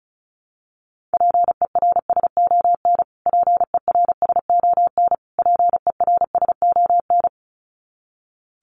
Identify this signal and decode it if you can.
Morse code, heard as a keyed tone, about 34 wpm, 700 Hz. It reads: PERSON PERSON PERSON
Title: Top 500 Words Repeated 35wpm